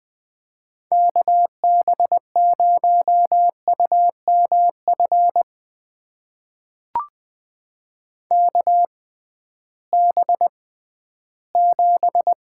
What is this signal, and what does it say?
Morse code, heard as a keyed tone, about 21 wpm, spaced as for 13 wpm, 700 Hz. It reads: KB0UMF KB7
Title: USA Call Signs - 6 Characters - 6x Character Spacing - 20wpm